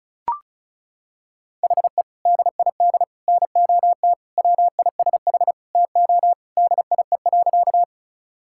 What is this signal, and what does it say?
Morse code, heard as a keyed tone, about 34 wpm, 700 Hz. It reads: HE DID NOT WISH TO DIE.
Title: Sentences from Top 1600 Words 35wpm